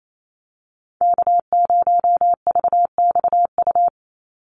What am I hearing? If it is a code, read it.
Morse code, heard as a keyed tone, about 28 wpm, 700 Hz. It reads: K0VXU
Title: Call Signs Repeated 3x - 28wpm